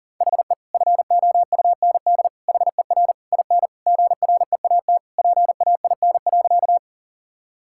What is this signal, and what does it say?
Morse code, heard as a keyed tone, about 40 wpm, 700 Hz. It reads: HE FOUND HER IN GREAT PAIN.
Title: Sentences from Top 1600 Words 40wpm